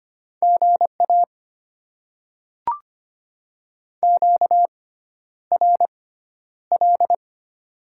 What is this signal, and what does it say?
Morse code, heard as a keyed tone, about 25 wpm, spaced as for 11 wpm, 700 Hz. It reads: GA QRL
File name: Top 20 QSO Elements - 6x Character Spacing - 25wpm